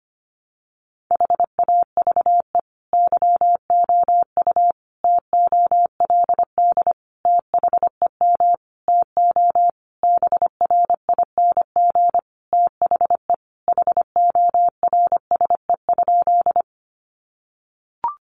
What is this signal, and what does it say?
Morse code, heard as a keyed tone, about 25 wpm, 700 Hz. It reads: HAVE YOU TOLD THEM TO BRING THE HORSE?
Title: Sentences from Top 500 Words 25wpm